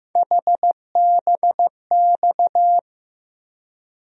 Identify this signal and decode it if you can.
Morse code, heard as a keyed tone, about 15 wpm, 700 Hz. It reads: HBX
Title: USA Call Sign Postfixes - 3 Letters 15wpm